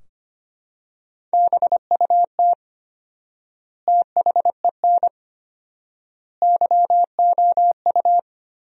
Morse code, heard as a keyed tone, about 25 wpm, 700 Hz. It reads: BUT THEN YOU
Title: 4x Word-Spacing - Sentences from Top 300 Words - 25wpm